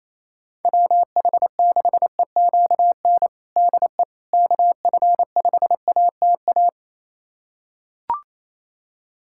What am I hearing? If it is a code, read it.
Morse code, heard as a keyed tone, about 28 wpm, 700 Hz. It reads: WH6EQN DE KF5ATA
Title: USA Callsign DE Callsign - 6 Characters - 28wpm